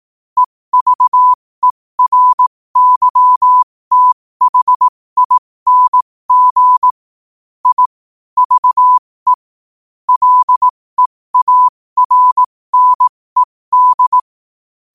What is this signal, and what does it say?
Morse code, heard as a keyed tone, about 18 wpm, spaced as for 15 wpm, 1000 Hz. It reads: EVERYTHING I VE LEARNED
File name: 15wpm